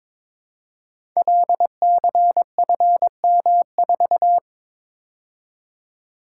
Morse code, heard as a keyed tone, about 22 wpm, 700 Hz. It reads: LCFM4
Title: Code Groups - 5 Letter:Numbers - 22wpm